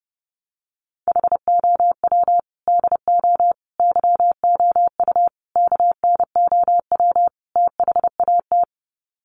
Morse code, heard as a keyed tone, about 30 wpm, 700 Hz. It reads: HOW DO YOU KNOW THAT
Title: Sets of 5 Words Encore 30wpm